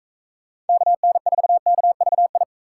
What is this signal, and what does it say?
Morse code, heard as a keyed tone, about 41 wpm, 700 Hz. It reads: KN4KVI